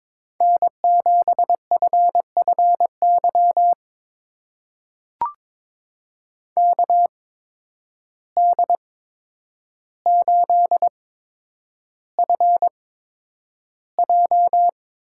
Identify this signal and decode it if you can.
Morse code, heard as a keyed tone, about 22 wpm, spaced as for 9 wpm, 700 Hz. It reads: N7FFY KD8FJ
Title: USA Call Signs - 8x Character Spacing - 22wpm